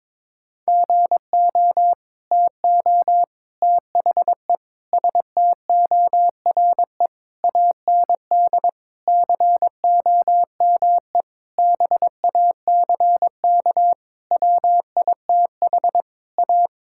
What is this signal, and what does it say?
Morse code, heard as a keyed tone, about 21 wpm, 700 Hz. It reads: GO TO THE STORE AND COME BACK WITH A
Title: Sentences from Top 1000 Words - Review Part 5 - 22wpm